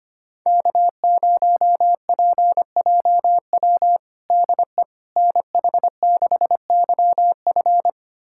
Morse code, heard as a keyed tone, about 25 wpm, 700 Hz. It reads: K0PJW DE NH6YF
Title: USA Callsign DE Callsign - 5 Characters - 25wpm